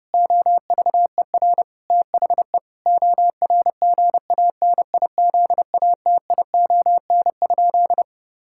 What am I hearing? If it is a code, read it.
Morse code, heard as a keyed tone, about 30 wpm, 700 Hz. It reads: OVER THE ORGANIZATION?
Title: Sentences from Top 1000 Words - Review Part 5 - 30wpm